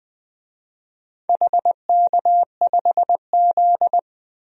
Morse code, heard as a keyed tone, about 20 wpm, 700 Hz. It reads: HK5Z
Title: DXCC-100 Call Signs - 4 Characters - 20wpm